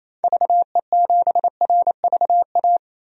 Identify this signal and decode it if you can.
Morse code, heard as a keyed tone, about 28 wpm, 700 Hz. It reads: VE7RVA
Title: DXCC-100 Call Signs - 6 Characters - 28wpm